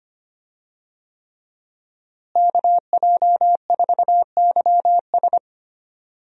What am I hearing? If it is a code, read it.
Morse code, heard as a keyed tone, about 25 wpm, 700 Hz. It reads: KJ4YS
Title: USA Call Signs - 5 Characters - Repeated 3x - 25wpm